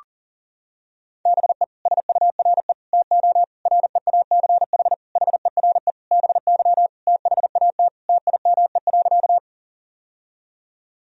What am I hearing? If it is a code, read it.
Morse code, heard as a keyed tone, about 40 wpm, 700 Hz. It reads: BE SURE TO REACH HERE BY THAT TIME.
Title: Sentences from Top 1000 Words - Review Part 2 - 40wpm